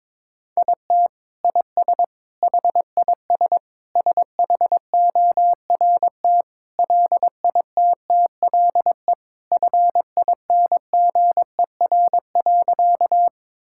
Morse code, heard as a keyed tone, about 22 wpm, 700 Hz. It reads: IT IS HIS SHORT LITTLE FINGER.
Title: Sentences from Top 1600 Words 22wpm